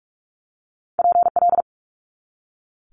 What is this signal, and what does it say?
Morse code, heard as a keyed tone, about 46 wpm, 700 Hz. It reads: PL
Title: Multi-letter Phonemes 45wpm